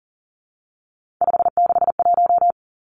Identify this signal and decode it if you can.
Morse code, heard as a keyed tone, about 38 wpm, 700 Hz. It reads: 561
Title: Numbers - 3-Digit 40wpm